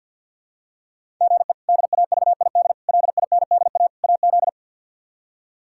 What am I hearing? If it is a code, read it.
Morse code, heard as a keyed tone, about 52 wpm, 700 Hz. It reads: GE DAVID LINDA AZ
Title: K1USN SST QSO Practice - 50wpm